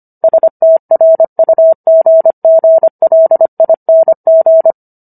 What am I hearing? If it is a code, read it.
Morse code, heard as a keyed tone, about 25 wpm, 650 Hz. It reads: STRUGGLING